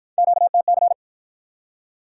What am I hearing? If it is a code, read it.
Morse code, heard as a keyed tone, about 49 wpm, 700 Hz. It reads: QTC